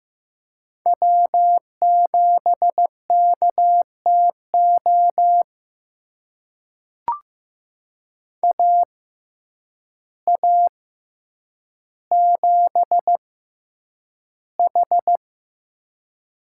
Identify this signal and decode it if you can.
Morse code, heard as a keyed tone, about 15 wpm, spaced as for 8 wpm, 700 Hz. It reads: W7KTO AA7H
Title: USA Call Signs - 6x Character Spacing - 15wpm